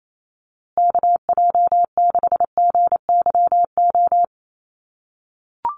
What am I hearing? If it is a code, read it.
Morse code, heard as a keyed tone, about 28 wpm, 700 Hz. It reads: KJ6GYO
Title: USA Call Signs Repeated 2x 28wpm